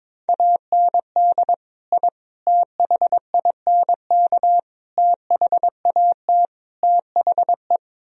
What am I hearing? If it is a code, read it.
Morse code, heard as a keyed tone, about 23 wpm, 700 Hz. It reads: AND I THINK THAT THE
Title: Speed-Racing - Sets of 5 Words 22wpm